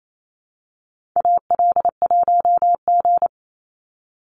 Morse code, heard as a keyed tone, about 28 wpm, 700 Hz. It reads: AL1G
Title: Call Signs Repeated 2x - 28wpm